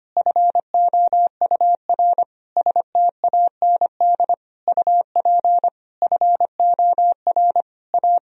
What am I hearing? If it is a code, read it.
Morse code, heard as a keyed tone, about 25 wpm, 700 Hz. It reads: FOUR STAND UP FOR A
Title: Sentences from Top 1000 Words - Review Part 1 - 25wpm